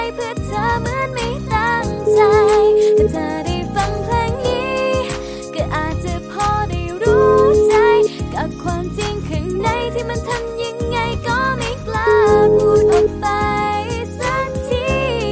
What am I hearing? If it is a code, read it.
Ringback tone (Europe, 425 Hz), with noise.